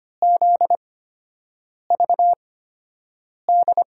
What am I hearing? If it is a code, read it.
Morse code, heard as a keyed tone, about 25 wpm, spaced as for 7 wpm, 700 Hz. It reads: ZVD
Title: USA Call Signs - 6 Characters - 8x Character Spacing - 25wpm